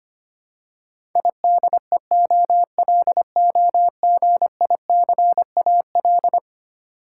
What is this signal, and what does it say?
Morse code, heard as a keyed tone, about 25 wpm, 700 Hz. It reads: IDEOLOGICAL
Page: morse-code-ninja-practice-25wpm